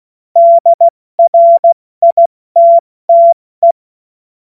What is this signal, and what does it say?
Morse code, heard as a keyed tone, about 16 wpm, spaced as for 14 wpm, 680 Hz. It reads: DRITTE
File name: Morsekurs / Witze / Witz 01. November 2022 70 F80